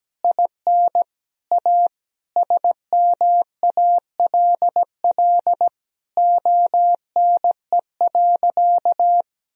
Morse code, heard as a keyed tone, about 17 wpm, 700 Hz. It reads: IN A SMALL ONE.
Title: Sentences from Top 1000 Words - Review Part 1 - 17wpm